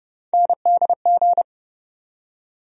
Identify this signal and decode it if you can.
Morse code, heard as a keyed tone, about 30 wpm, 700 Hz. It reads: NDG
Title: USA Call Sign Postfixes - 3 Letters 30wpm